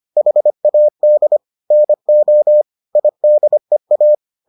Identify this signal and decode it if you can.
Morse code, heard as a keyed tone, about 25 wpm, 600 Hz. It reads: HAD NO IDEA